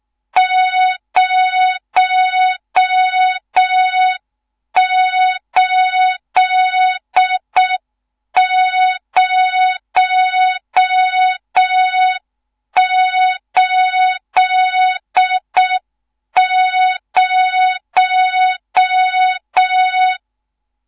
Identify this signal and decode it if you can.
Morse code, heard as a keyed tone, about 6 wpm, 3700 Hz, with noise. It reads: 08080
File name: Morze-HA2MN / hang / 14 280 HA2MN